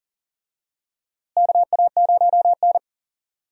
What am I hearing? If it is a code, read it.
Morse code, heard as a keyed tone, about 40 wpm, 700 Hz. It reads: KA0N
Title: USA Call Signs - 4 Characters - Repeated 3x - 40wpm